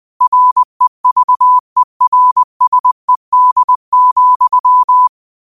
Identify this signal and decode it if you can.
Morse code, heard as a keyed tone, about 20 wpm, 1000 Hz. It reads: REVERSED,
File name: qotdfiles / 20wpm